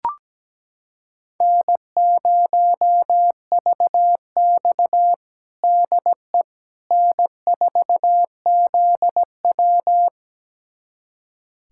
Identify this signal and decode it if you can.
Morse code, heard as a keyed tone, about 17 wpm, 700 Hz. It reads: N0VX DE N4ZW
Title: USA Callsign DE Callsign - 4 Characters - 17wpm